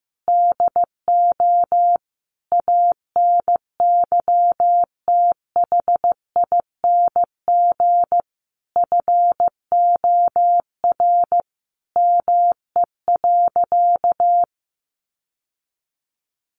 Morse code, heard as a keyed tone, about 15 wpm, 700 Hz. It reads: DO ANYTHING FOR ME.
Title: Sentences from Top 1000 Words 15wpm